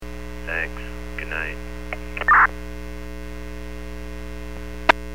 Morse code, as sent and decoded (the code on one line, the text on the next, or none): none